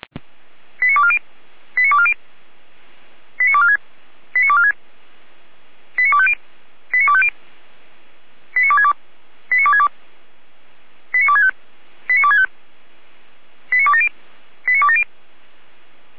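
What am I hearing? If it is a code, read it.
Five-tone selective call (ZVEI), heard as quick runs of five tones, tones of about 70 ms. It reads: 89240 89240 89246 89246 89250 89250 89262 89262 89265 89265 89270 89270
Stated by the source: ZVEI five-tone selective call received with an SDR, hosted by Feuerwehr Orlamünde (fire brigade).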